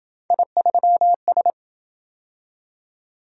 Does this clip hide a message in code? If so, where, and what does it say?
Yes: Morse code from 0.3 to 1.5 s, heard as a keyed tone, about 27 wpm, 700 Hz. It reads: I3S